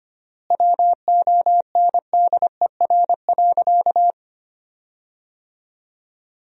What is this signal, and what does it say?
Morse code, heard as a keyed tone, about 25 wpm, 700 Hz. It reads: WONDER.
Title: 3x Word-Spacing - Sentences from Top 500 Words - 25wpm